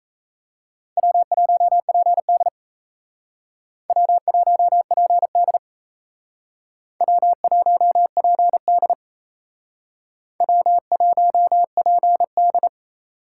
Morse code, heard as a keyed tone, about 36 wpm, 700 Hz. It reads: W1PD W1PD W1PD W1PD